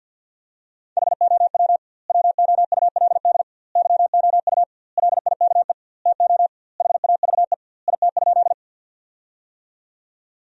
Morse code, heard as a keyed tone, about 50 wpm, 700 Hz. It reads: HOW WOULD YOU LIKE TO HAVE IT?
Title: 2x Word-Spacing - Sentences from Top 100 Words - 50wpm